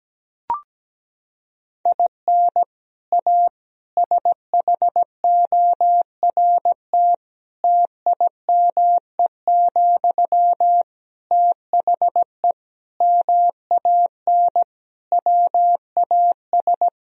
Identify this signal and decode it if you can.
Morse code, heard as a keyed tone, about 17 wpm, 700 Hz. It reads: IN A SHORT TIME, THE MAN WAS
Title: Sentences from Top 1000 Words - Review Part 2 - 17wpm